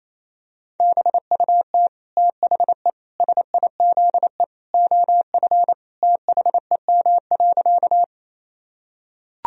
Morse code, heard as a keyed tone, about 28 wpm, 700 Hz. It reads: BUT THE SIZE OF THEM.
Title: Sentences from Top 1000 Words - Review Part 2 - 28wpm